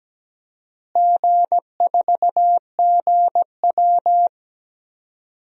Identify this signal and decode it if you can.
Morse code, heard as a keyed tone, about 17 wpm, 700 Hz. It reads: G4GW
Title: DXCC-100 Call Signs - 4 Characters - Repeated 3x - 17wpm